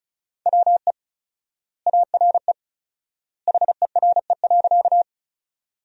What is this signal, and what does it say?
Morse code, heard as a keyed tone, about 34 wpm, 700 Hz. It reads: WE ARE HERE.
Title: 4x Word-Spacing - Sentences from Top 200 Words - 35wpm